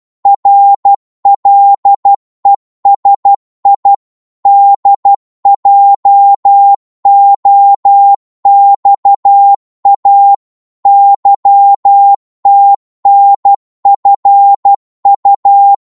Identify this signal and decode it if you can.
Morse code, heard as a keyed tone, about 12 wpm, 800 Hz. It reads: RLESI DJOXA YTNFU
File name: Morse / ltr-12wpm